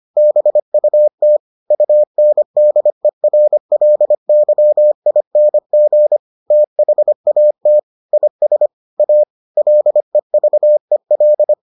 Morse code, heard as a keyed tone, about 25 wpm, 600 Hz. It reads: BUT UNDERLYING THAT IS A LEVEL